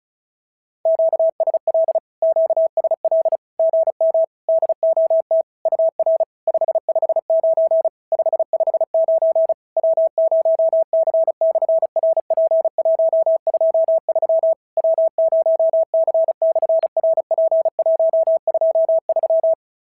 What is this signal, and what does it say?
Morse code, heard as a keyed tone, about 36 wpm, 650 Hz. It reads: QSL QSL GM DOT UR 559 559 W0C/RP123 W0C/RP123
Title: SOTA QSO Practice 35wpm